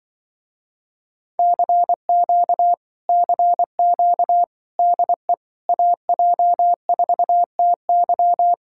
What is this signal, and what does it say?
Morse code, heard as a keyed tone, about 23 wpm, 700 Hz. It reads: CQ CQ DE AJ4TY
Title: Speed-Racing Common QSO Phrases - 20wpm